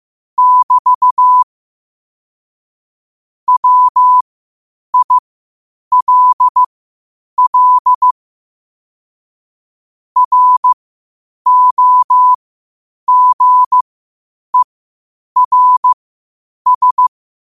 Morse code, heard as a keyed tone, about 15 wpm, spaced as for 8 wpm, 1000 Hz. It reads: <BT> WILL ROGERS
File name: qotdfiles / 8wpm